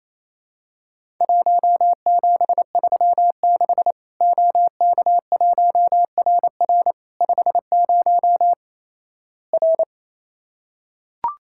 Morse code, heard as a keyed tone, about 27 wpm, 700 Hz. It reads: 1736 OK1RR 50 R
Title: Worked All Europe DX Contest - QTC Exchange - 28wpm